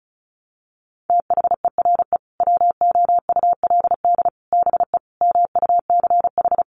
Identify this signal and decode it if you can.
Morse code, heard as a keyed tone, about 34 wpm, 700 Hz. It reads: THERE WOULD BE MUCH
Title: Sentences from Top 1200 Words 35wpm